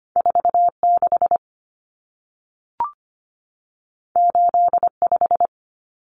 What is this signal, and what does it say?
Morse code, heard as a keyed tone, about 25 wpm, 700 Hz. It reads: 46 85
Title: Common Cut Numbers - 2-Digit 25wpm